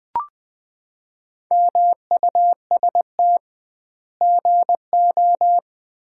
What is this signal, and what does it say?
Morse code, heard as a keyed tone, about 20 wpm, 700 Hz. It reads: MUST GO
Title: Common QSO Phrases - 2x Word-Spacing - 20wpm